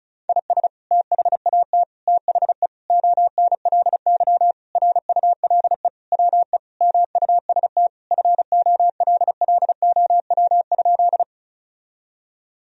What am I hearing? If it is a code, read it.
Morse code, heard as a keyed tone, about 36 wpm, 700 Hz. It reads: IS THAT THE ONLY RULE WE MUST FOLLOW?
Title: Sentences from Top 1000 Words - Review Part 2 - 35wpm